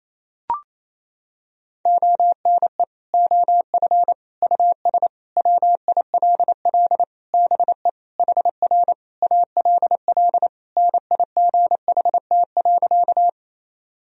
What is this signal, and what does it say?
Morse code, heard as a keyed tone, about 28 wpm, 700 Hz. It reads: ONE OF US WILL BE HR ALL NIGHT.
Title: Sentences from Top 300 Words and Abbreviations 28wpm